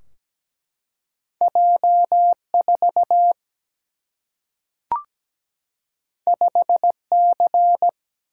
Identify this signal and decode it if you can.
Morse code, heard as a keyed tone, about 17 wpm, 700 Hz. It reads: J4 5C
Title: Single-Letter Single-Number 17wpm